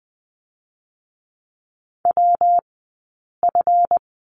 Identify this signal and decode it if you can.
Morse code, heard as a keyed tone, about 20 wpm, 700 Hz. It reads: W F
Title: 137 Lesson 23d - Letter f - Two Characters 20wpm